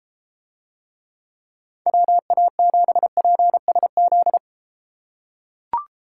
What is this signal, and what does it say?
Morse code, heard as a keyed tone, about 31 wpm, 700 Hz. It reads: WA7PSZ